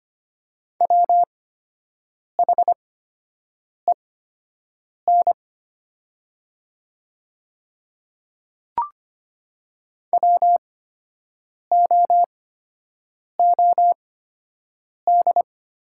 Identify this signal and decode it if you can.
Morse code, heard as a keyed tone, about 25 wpm, spaced as for 7 wpm, 700 Hz. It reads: WHEN WOOD
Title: 8x Character Spacing - Top 500 Words - 25wpm